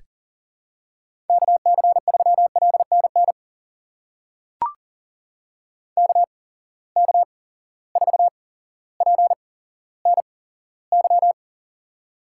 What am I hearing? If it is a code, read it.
Morse code, heard as a keyed tone, about 40 wpm, spaced as for 17 wpm, 700 Hz. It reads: KC3LNN KK4PNY
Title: USA Call Signs - 6 Characters - 8x Character Spacing - 40wpm